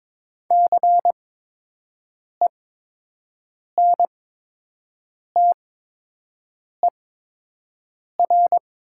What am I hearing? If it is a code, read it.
Morse code, heard as a keyed tone, about 22 wpm, spaced as for 6 wpm, 700 Hz. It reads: CENTER
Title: 8x Character Spacing - Top 500 Words - 22wpm